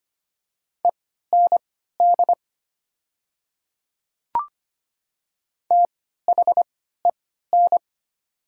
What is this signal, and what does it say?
Morse code, heard as a keyed tone, about 25 wpm, spaced as for 14 wpm, 700 Hz. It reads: END THEN
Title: 3x Character Spacing - Top 200 Words - 25wpm